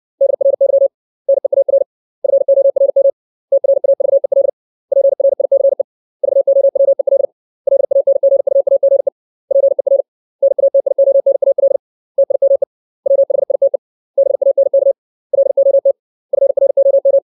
Morse code, heard as a keyed tone, about 60 wpm, 550 Hz. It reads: DRY NEAR FORM TRAVEL PRICE VOWEL LANGUAGE PER NATIONAL TIME WHITE BANK LOT FROM
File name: 632 words / 632 Words 60WPM